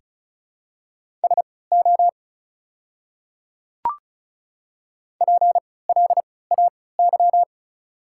Morse code, heard as a keyed tone, about 36 wpm, spaced as for 20 wpm, 700 Hz. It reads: SO PLAY
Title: 3x Character Spacing - Top 200 Words - 35wpm